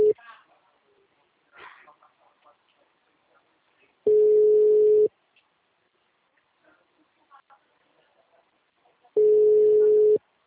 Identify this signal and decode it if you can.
Ringback tone (Europe, 425 Hz), with speech.